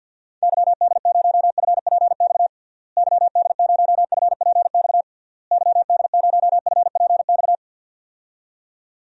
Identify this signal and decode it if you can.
Morse code, heard as a keyed tone, about 48 wpm, 700 Hz. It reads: YD0FPX YD0FPX YD0FPX
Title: DXCC-100 Call Signs - 6 Characters - Repeated 3x - 50wpm